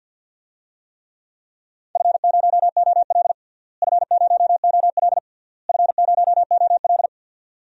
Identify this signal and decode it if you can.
Morse code, heard as a keyed tone, about 46 wpm, 700 Hz. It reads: F0OL F0OL F0OL
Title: DXCC-100 Call Signs - 4 Characters - Repeated 3x - 50wpm